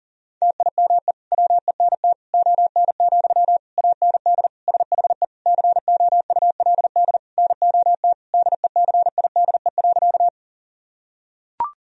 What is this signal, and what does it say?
Morse code, heard as a keyed tone, about 40 wpm, 700 Hz. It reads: TIME WENT ON, AND SHE COULD NOT DECIDE.
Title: Sentences from Top 1000 Words - Review Part 3 - 40wpm